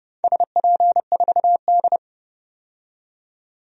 Morse code, heard as a keyed tone, about 30 wpm, 700 Hz. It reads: SP4D
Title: DXCC-100 Call Signs - 4 Characters - 30wpm